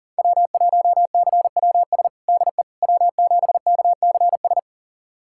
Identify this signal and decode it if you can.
Morse code, heard as a keyed tone, about 40 wpm, 700 Hz. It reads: W1CWS DE W7KCS
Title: USA Callsign DE Callsign - 5 Characters - 40wpm